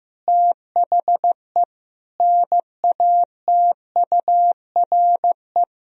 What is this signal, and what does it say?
Morse code, heard as a keyed tone, about 16 wpm, 700 Hz. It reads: THE NATURE
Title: Sentences from Top 1000 Words - Review Part 4 - 15wpm